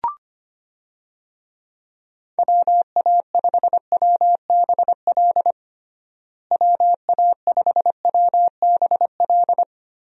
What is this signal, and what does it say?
Morse code, heard as a keyed tone, about 25 wpm, 700 Hz. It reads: WA5WBL WA5WBL
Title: USA Call Signs - 6 Characters - Repeated 3x - 25wpm